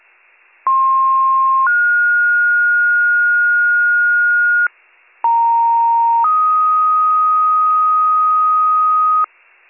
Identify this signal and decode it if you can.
Two-tone page, heard as a one-second tone, then a three-second tone, about 1060 then 1430 Hz, about 930 then 1250 Hz, with noise. It is heard from 0.7 to 9.2 s.